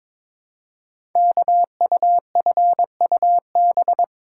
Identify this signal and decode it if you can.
Morse code, heard as a keyed tone, about 22 wpm, 700 Hz. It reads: KUFUB
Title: Code Groups - 5 Letters - 22wpm